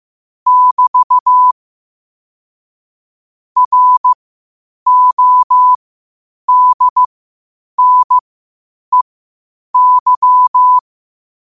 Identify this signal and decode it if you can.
Morse code, heard as a keyed tone, about 15 wpm, spaced as for 8 wpm, 1000 Hz. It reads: <BT> RODNEY
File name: qotdfiles / 8wpm